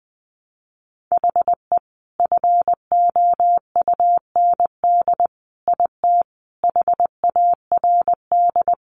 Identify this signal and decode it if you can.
Morse code, heard as a keyed tone, about 19 wpm, 700 Hz. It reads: HE FOUND IT HARD
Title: Sentences from Top 1400 Words 20wpm